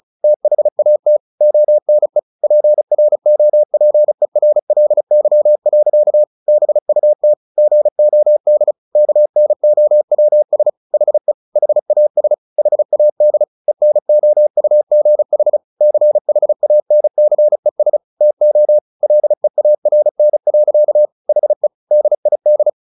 Morse code, heard as a keyed tone, about 34 wpm, 600 Hz. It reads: THAT ONE PROPERLY. BUT GOD KNOWS HE HAS HAD ENOUGH CHANCES TO LEARN. HE DID
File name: the old man and the sea 2 35wpm